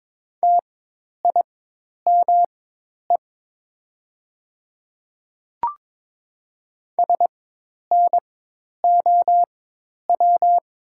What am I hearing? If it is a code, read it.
Morse code, heard as a keyed tone, about 21 wpm, spaced as for 10 wpm, 700 Hz. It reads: TIME SNOW
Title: 4x Character Spacing - Top 500 Words - 22wpm